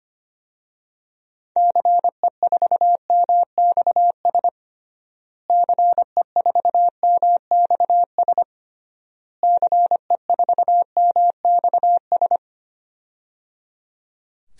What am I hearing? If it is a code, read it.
Morse code, heard as a keyed tone, about 25 wpm, 700 Hz. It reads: CE4MXS CE4MXS CE4MXS
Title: DXCC-100 Call Signs - 6 Characters - Repeated 3x - 25wpm